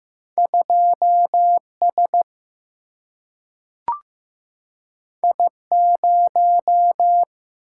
Morse code, heard as a keyed tone, about 15 wpm, 700 Hz. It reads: 2 S I 0
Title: Single-Letter Single-Number 15wpm